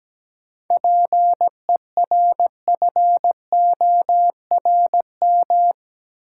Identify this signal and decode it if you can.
Morse code, heard as a keyed tone, about 17 wpm, 700 Hz. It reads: PERFORM
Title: Sentences from Top 1000 Words - Review Part 5 - New Word Drill - 17wpm